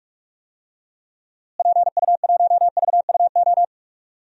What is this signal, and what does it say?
Morse code, heard as a keyed tone, about 42 wpm, 700 Hz. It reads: PU1VUO
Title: DXCC-100 Call Signs - 6 Characters - Repeated 3x - 45wpm